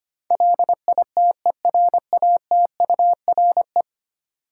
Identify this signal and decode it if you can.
Morse code, heard as a keyed tone, about 25 wpm, 700 Hz. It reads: LITERATURE
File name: Ten Letter Words - 25wpm